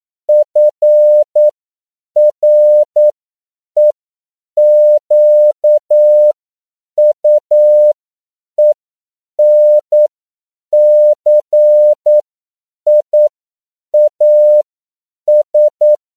Morse code, heard as a keyed tone, about 9 wpm, spaced as for 7 wpm, 600 Hz. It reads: FREQUENCIAS
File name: CW / 10PPM